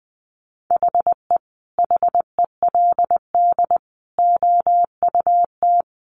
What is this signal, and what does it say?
Morse code, heard as a keyed tone, about 20 wpm, 700 Hz. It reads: HE HELD OUT
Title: A Sound Of Thunder 20wpm